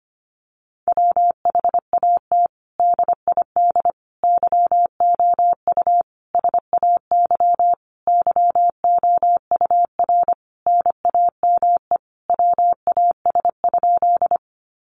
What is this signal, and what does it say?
Morse code, heard as a keyed tone, about 25 wpm, 700 Hz. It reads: WHAT DID YOU SAY YOUR NAME WAS?
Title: Sentences from Top 200 Words 25wpm